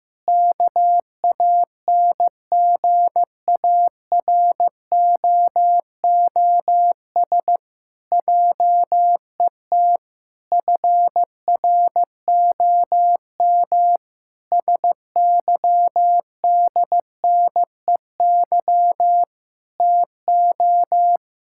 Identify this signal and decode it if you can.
Morse code, heard as a keyed tone, about 15 wpm, 700 Hz. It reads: KANGAROOS JET FROM SYDNEY TO
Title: Pangram Challenge v2 - 15wpm